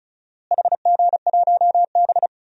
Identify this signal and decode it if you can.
Morse code, heard as a keyed tone, about 36 wpm, 700 Hz. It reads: HG1B
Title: DXCC-100 Call Signs - 4 Characters - Repeated 3x - 35wpm